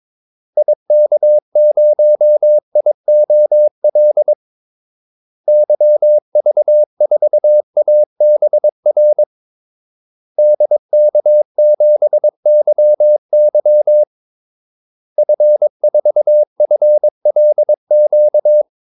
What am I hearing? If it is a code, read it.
Morse code, heard as a keyed tone, about 22 wpm, 600 Hz. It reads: IK0IOL YV4ABR DK7YY F4FLQ
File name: Callsigns-22wpm